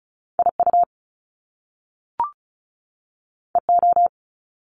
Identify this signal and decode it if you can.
Morse code, heard as a keyed tone, about 36 wpm, 700 Hz. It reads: IU EO
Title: Top Two Letter Combinations in Words 35wpm